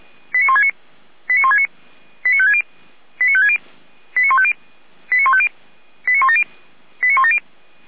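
Five-tone selective call (ZVEI), heard as quick runs of five tones, tones of about 70 ms. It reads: 89169 89169 89560 89560 89150 89150 89170 89170
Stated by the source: ZVEI five-tone call received by SDR, archived on the server of Feuerwehr Orlamünde.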